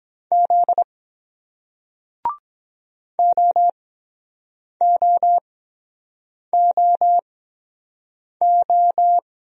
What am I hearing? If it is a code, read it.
Morse code, heard as a keyed tone, about 23 wpm, spaced as for 7 wpm, 700 Hz. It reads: Z OOOO